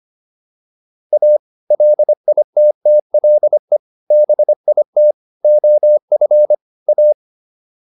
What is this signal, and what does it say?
Morse code, heard as a keyed tone, about 25 wpm, 600 Hz. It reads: A LITTLE BIT OF A